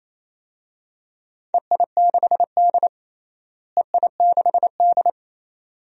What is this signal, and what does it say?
Morse code, heard as a keyed tone, about 28 wpm, 700 Hz. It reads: EI6D EI6D
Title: DXCC-100 Call Signs - 4 Characters - Repeated 2x - 28wpm